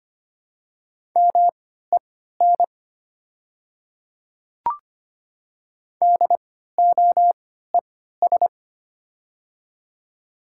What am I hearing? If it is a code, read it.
Morse code, heard as a keyed tone, about 25 wpm, spaced as for 14 wpm, 700 Hz. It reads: MEN DOES
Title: 3x Character Spacing - Top 200 Words - 25wpm